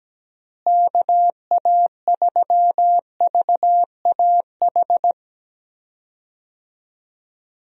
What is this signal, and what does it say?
Morse code, heard as a keyed tone, about 17 wpm, 700 Hz. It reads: KA3VAH
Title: USA Call Signs - 6 Characters - Repeated 2x - 17wpm